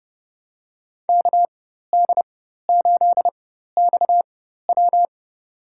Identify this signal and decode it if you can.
Morse code, heard as a keyed tone, about 30 wpm, spaced as for 14 wpm, 700 Hz. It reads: KD8XW
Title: USA Call Signs - 5 Characters - 4x Character Spacing - 30wpm